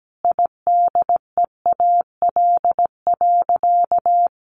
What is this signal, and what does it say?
Morse code, heard as a keyed tone, about 17 wpm, 700 Hz. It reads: IDEAL.
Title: Sentences from Top 1400 Words 17wpm